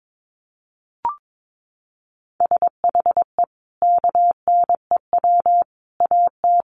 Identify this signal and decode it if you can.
Morse code, heard as a keyed tone, about 22 wpm, 700 Hz. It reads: SHE KNEW AT
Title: Sentences from Top 400 Words 22wpm